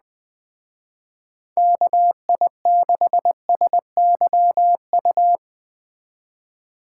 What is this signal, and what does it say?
Morse code, heard as a keyed tone, about 21 wpm, 700 Hz. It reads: KI6SYU
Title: Speed Racing - USA 2x3 Call Signs 17wpm